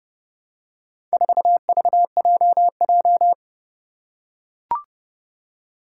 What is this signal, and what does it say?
Morse code, heard as a keyed tone, about 30 wpm, 700 Hz. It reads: 4VJJ
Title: Four Challenging Letter-Number Combinations 30wpm